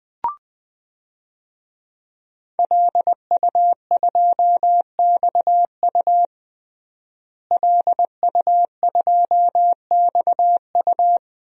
Morse code, heard as a keyed tone, about 20 wpm, 700 Hz. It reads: LU2XU LU2XU
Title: DXCC-100 Call Signs - 5 Characters - Repeated 2x - 20wpm